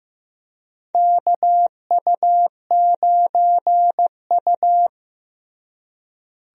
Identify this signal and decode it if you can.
Morse code, heard as a keyed tone, about 15 wpm, 700 Hz. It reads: KU9U
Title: USA Call Signs - 4 Characters - Repeated 3x - 15wpm